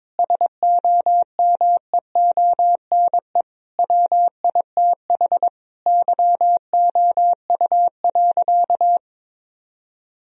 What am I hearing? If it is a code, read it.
Morse code, heard as a keyed tone, about 22 wpm, 700 Hz. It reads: SOMEONE WITH YOU.
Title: Sentences from Top 1000 Words - Review Part 4 - 22wpm